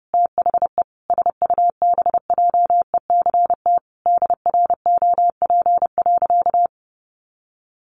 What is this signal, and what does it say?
Morse code, heard as a keyed tone, about 30 wpm, 700 Hz. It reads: THE SUBJECT DROP.
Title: Sentences from Top 1400 Words 30wpm